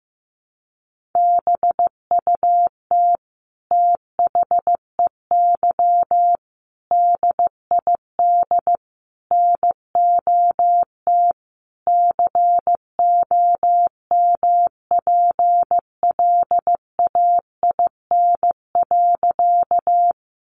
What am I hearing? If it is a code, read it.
Morse code, heard as a keyed tone, about 15 wpm, 700 Hz. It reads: BUT THEY DID NOT COMPLAIN.
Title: Sentences from Top 1300 Words 15wpm